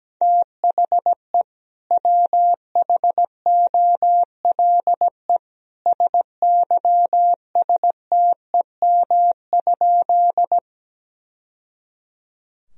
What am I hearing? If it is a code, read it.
Morse code, heard as a keyed tone, about 17 wpm, 700 Hz. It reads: THE WHOLE SYSTEM?
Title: Sentences from Top 1000 Words - Review Part 3 - 17wpm